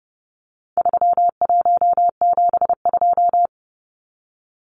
Morse code, heard as a keyed tone, about 29 wpm, 700 Hz. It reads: 3172
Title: Numbers - 4-Digit 30wpm